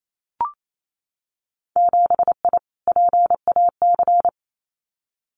Morse code, heard as a keyed tone, about 28 wpm, 700 Hz. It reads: 7I PAC
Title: Winter Field Day Exchanges Encore 28wpm